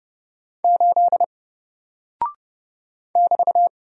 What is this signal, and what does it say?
Morse code, heard as a keyed tone, about 30 wpm, 700 Hz. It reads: 8 <BT>
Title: Single Character (Letter:Number:Punctuation:Prosign) Alt 30wpm